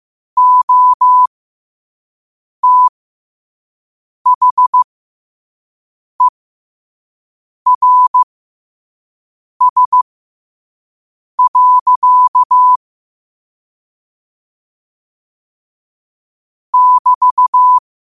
Morse code, heard as a keyed tone, about 15 wpm, spaced as for 5 wpm, 1000 Hz. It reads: OTHERS. <BT>
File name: qotdfiles / 5wpm